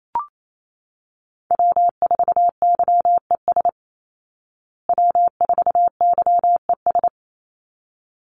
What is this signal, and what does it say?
Morse code, heard as a keyed tone, about 28 wpm, 700 Hz. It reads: W4YES W4YES
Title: Call Signs Repeated 3x - 28wpm